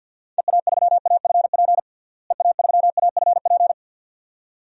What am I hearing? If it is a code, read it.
Morse code, heard as a keyed tone, about 52 wpm, 700 Hz. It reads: EA3AFP EA3AFP
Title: DXCC-100 Call Signs - 6 Characters - Repeated 2x - 50wpm